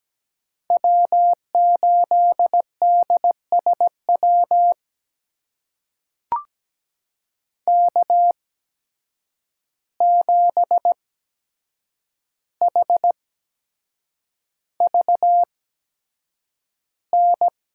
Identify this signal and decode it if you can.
Morse code, heard as a keyed tone, about 17 wpm, spaced as for 7 wpm, 700 Hz. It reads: W8DSW K7HVN
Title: USA Call Signs - 5 Characters - 8x Character Spacing - 17wpm